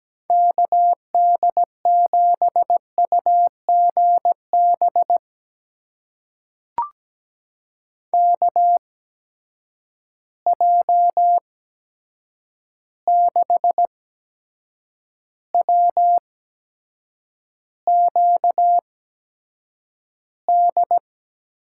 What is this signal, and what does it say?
Morse code, heard as a keyed tone, about 17 wpm, spaced as for 7 wpm, 700 Hz. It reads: KD7UGB KJ6WQD
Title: USA Call Signs - 6 Characters - 8x Character Spacing - 17wpm